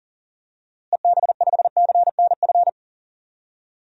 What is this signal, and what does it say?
Morse code, heard as a keyed tone, about 37 wpm, 700 Hz. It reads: EB5CNF
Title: DXCC-100 Call Signs - 6 Characters - Repeated 2x - 40wpm